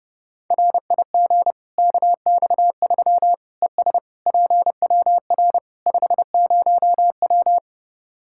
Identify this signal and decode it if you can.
Morse code, heard as a keyed tone, about 30 wpm, 700 Hz. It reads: RIG KX3 ES PWR 50W
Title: Practice QSO - Exchange 2 - Info - 30wpm